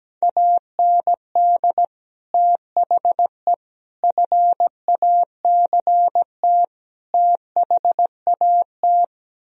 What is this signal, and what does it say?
Morse code, heard as a keyed tone, about 17 wpm, 700 Hz. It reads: AND THE FACT THAT
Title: Speed-Racing - Sets of 4 Words 17wpm